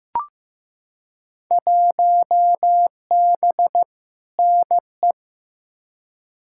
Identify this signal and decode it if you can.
Morse code, heard as a keyed tone, about 16 wpm, 700 Hz. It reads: 1B NE
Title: Field Day Exchanges Encore - 15wpm